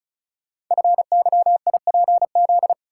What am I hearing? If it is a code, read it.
Morse code, heard as a keyed tone, about 36 wpm, 700 Hz. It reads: FYIPZ
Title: Code Groups - 5 Letter:Numbers - 35wpm